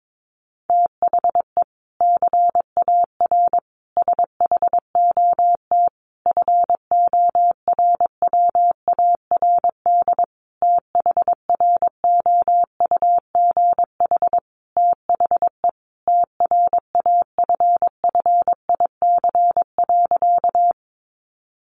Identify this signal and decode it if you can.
Morse code, heard as a keyed tone, about 22 wpm, 700 Hz. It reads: THE CAR SHOT FORWARD THROUGH THE TRAFFIC.
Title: Sentences from Top 1300 Words 22wpm